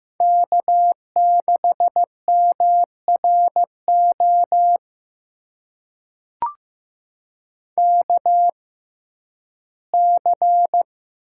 Morse code, heard as a keyed tone, about 15 wpm, spaced as for 11 wpm, 700 Hz. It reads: K6MRO KC
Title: USA Call Signs - 6x Character Spacing - 15wpm